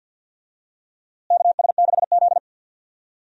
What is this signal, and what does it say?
Morse code, heard as a keyed tone, about 50 wpm, 700 Hz. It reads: KS6Z
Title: USA Call Signs - 4 Characters - Repeated 2x - 50wpm